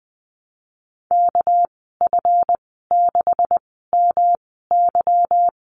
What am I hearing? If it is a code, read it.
Morse code, heard as a keyed tone, about 20 wpm, spaced as for 15 wpm, 700 Hz. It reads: KF6MY
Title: 207 Lesson 33g - Letter j - Call Signs 20wpm